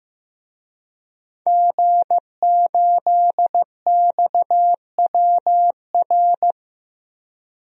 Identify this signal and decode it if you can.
Morse code, heard as a keyed tone, about 15 wpm, 700 Hz. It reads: G8XWR
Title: DXCC-100 Call Signs - 5 Characters - 15wpm